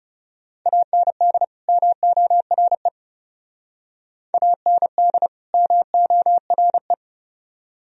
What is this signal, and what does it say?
Morse code, heard as a keyed tone, about 38 wpm, 700 Hz. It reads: AND MORE AND MORE